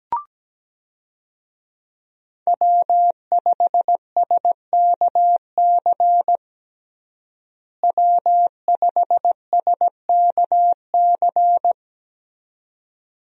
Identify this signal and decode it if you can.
Morse code, heard as a keyed tone, about 17 wpm, 700 Hz. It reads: W5SKC W5SKC
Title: USA Call Signs - 5 Characters - Repeated 2x - 17wpm